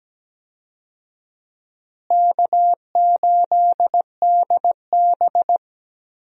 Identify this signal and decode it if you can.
Morse code, heard as a keyed tone, about 17 wpm, 700 Hz. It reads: K8DB
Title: USA Call Signs - 4 Characters - Repeated 2x - 17wpm